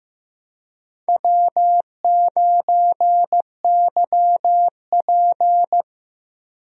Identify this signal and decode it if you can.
Morse code, heard as a keyed tone, about 15 wpm, 700 Hz. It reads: W9YP
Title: USA Call Signs - 4 Characters - Repeated 3x - 15wpm